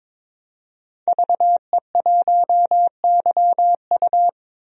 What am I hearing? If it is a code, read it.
Morse code, heard as a keyed tone, about 22 wpm, 700 Hz. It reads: VE1YU
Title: DXCC-100 Call Signs - 5 Characters - 22wpm